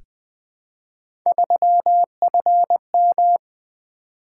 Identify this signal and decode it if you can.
Morse code, heard as a keyed tone, about 20 wpm, 700 Hz. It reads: 3FM
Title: Three Letter-Number Combinations - 20wpm